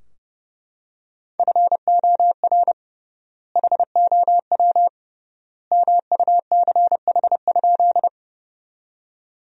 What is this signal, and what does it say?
Morse code, heard as a keyed tone, about 30 wpm, 700 Hz. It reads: FOR HOW MUCH?
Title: 3x Word-Spacing - Sentences from Top 200 Words - 30wpm